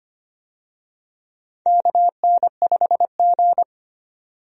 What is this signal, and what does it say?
Morse code, heard as a keyed tone, about 25 wpm, 700 Hz. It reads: KN5G
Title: USA Call Signs - 4 Characters - 25wpm